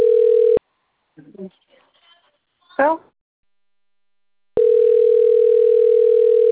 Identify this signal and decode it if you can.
Ringback tone (North America), 440 + 480 Hz, with speech.
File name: rg-1600-unknown